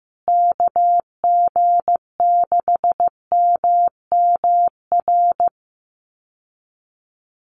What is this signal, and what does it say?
Morse code, heard as a keyed tone, about 15 wpm, 700 Hz. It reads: KG6MMR
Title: USA Call Signs Repeated 3x 15wpm